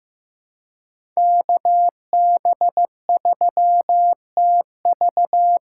Morse code, heard as a keyed tone, about 15 wpm, 700 Hz. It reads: KB3TV
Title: USA Call Signs - 6 Characters - 6x Character Spacing - 15wpm